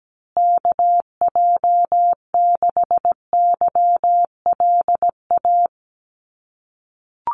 Morse code, heard as a keyed tone, about 17 wpm, 700 Hz. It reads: KJ6YLA
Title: USA Call Signs Repeated 2x 17wpm